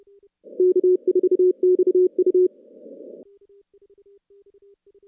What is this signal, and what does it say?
Morse code, heard as a keyed tone, about 30 wpm, 370 Hz. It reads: K4XU
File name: contests / CW-SS-k5tr- -2100